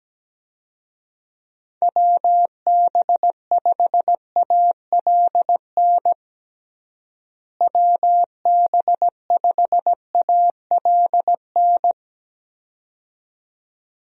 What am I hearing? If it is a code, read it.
Morse code, heard as a keyed tone, about 17 wpm, 700 Hz. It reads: WB5ALN WB5ALN
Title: USA Call Signs - 6 Characters - Repeated 2x - 17wpm